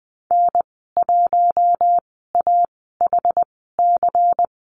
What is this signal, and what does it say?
Morse code, heard as a keyed tone, about 20 wpm, spaced as for 15 wpm, 700 Hz. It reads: N1AHC
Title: 147 Lesson 24g - Letter y - Call Signs 20wpm